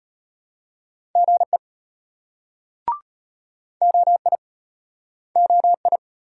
Morse code, heard as a keyed tone, about 34 wpm, 700 Hz. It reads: GE OI OI